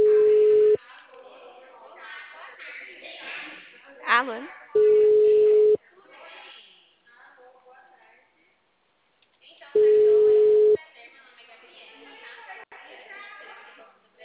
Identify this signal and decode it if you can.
Ringback tone (Europe, 425 Hz), with speech.
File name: out- -2044- - - .75672